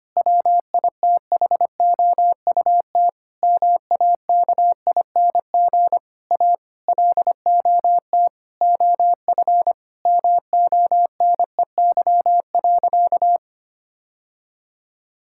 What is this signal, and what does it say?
Morse code, heard as a keyed tone, about 25 wpm, 700 Hz. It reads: WITHOUT MAKING A LOT OF MONEY.
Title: Sentences from Top 1000 Words - Review Part 6 - 25wpm